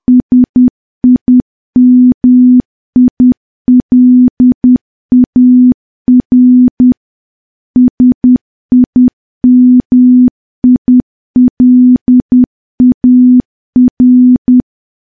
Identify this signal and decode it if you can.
Morse code, heard as a keyed tone, about 10 wpm, 250 Hz. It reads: SIMILAR SIMILAR
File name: morse / morse-training-10WPM